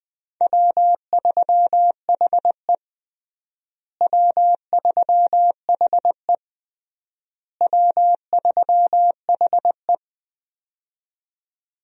Morse code, heard as a keyed tone, about 21 wpm, 700 Hz. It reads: W3HE W3HE W3HE
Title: USA Call Signs - 4 Characters - Repeated 3x - 20wpm